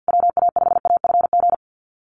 Morse code, heard as a keyed tone, about 53 wpm, 700 Hz. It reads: WA5AFG
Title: USA Call Signs Repeated 2x 50wpm